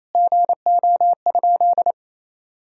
Morse code, heard as a keyed tone, about 28 wpm, 700 Hz. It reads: GO?
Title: Sentences from Top 1000 Words - Review Part 4 - 28wpm